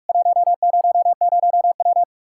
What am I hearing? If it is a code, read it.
Morse code, heard as a keyed tone, about 45 wpm, 700 Hz. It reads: 100W
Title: K1USN SST QSO Practice 45wpm